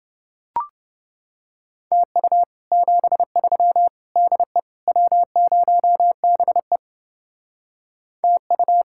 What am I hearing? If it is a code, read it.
Morse code, heard as a keyed tone, about 29 wpm, 700 Hz. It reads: TU 73 DE W0BE TU